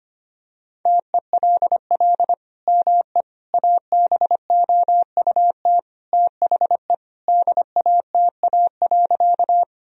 Morse code, heard as a keyed tone, about 25 wpm, 700 Hz. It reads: TELL ME ABOUT THE DATA.
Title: Sentences from Top 1000 Words - Review Part 3 - 25wpm